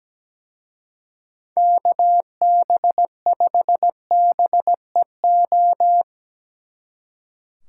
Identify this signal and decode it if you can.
Morse code, heard as a keyed tone, about 19 wpm, 700 Hz. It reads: KB5BEO
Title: Speed Racing - USA 2x3 Call Signs 17wpm